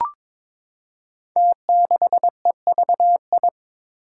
Morse code, heard as a keyed tone, about 22 wpm, 700 Hz. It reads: T6EVI
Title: Five Challenging Letter-Number Combinations 22wpm